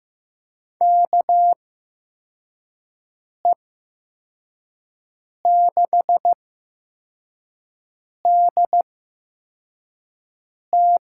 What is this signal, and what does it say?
Morse code, heard as a keyed tone, about 15 wpm, spaced as for 4 wpm, 700 Hz. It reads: KE6DT
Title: USA Call Signs - 8x Character Spacing - 15wpm